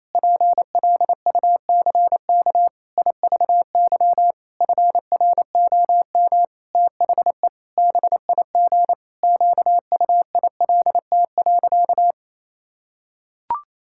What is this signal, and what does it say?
Morse code, heard as a keyed tone, about 28 wpm, 700 Hz. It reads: PLUCK IVY FROM THE BIG QUILT.
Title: Pangram Challenge v2 - 28wpm